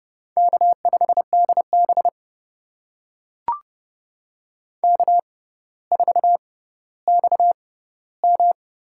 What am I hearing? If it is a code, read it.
Morse code, heard as a keyed tone, about 31 wpm, spaced as for 16 wpm, 700 Hz. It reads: K5DB K4XM
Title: USA Call Signs - 4 Characters - 6x Character Spacing - 30wpm